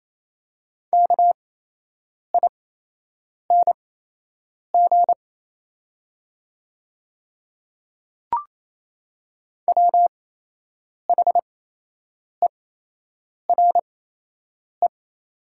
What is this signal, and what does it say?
Morse code, heard as a keyed tone, about 28 wpm, spaced as for 8 wpm, 700 Hz. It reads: KING WHERE
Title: 8x Character Spacing - Top 500 Words - 28wpm